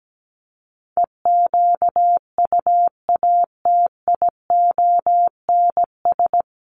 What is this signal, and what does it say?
Morse code, heard as a keyed tone, about 17 wpm, 700 Hz. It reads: EQUATIONS
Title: Sets of 5 Words 17wpm